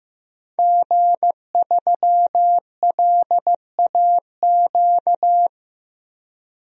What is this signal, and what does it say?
Morse code, heard as a keyed tone, about 15 wpm, 700 Hz. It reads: G3LAQ
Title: DXCC-100 Call Signs - 5 Characters - 15wpm